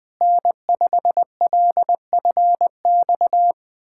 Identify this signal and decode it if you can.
Morse code, heard as a keyed tone, about 20 wpm, 700 Hz. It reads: N5LFX
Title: USA Callsign DE Callsign 20wpm